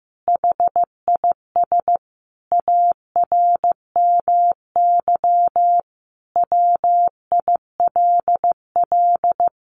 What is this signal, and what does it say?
Morse code, heard as a keyed tone, about 15 wpm, 700 Hz. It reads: HIS ARMY WILL
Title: Sentences from Top 1500 Words 15wpm